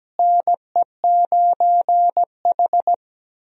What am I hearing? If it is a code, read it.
Morse code, heard as a keyed tone, about 17 wpm, 700 Hz. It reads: NE9H
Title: USA Call Signs - 4 Characters - Repeated 3x - 17wpm